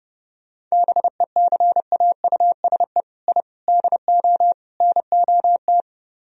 Morse code, heard as a keyed tone, about 34 wpm, 700 Hz. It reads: BECAUSE I DO NOT
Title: Speed-Racing - Sets of 4 Words 30wpm